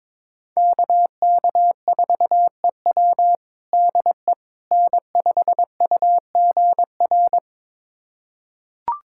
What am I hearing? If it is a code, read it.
Morse code, heard as a keyed tone, about 22 wpm, 700 Hz. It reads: KK4EW DE N5UGR
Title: USA Callsign DE Callsign - 5 Characters - 22wpm